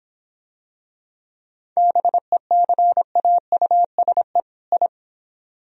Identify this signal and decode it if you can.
Morse code, heard as a keyed tone, about 24 wpm, 700 Hz. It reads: BECAUSE I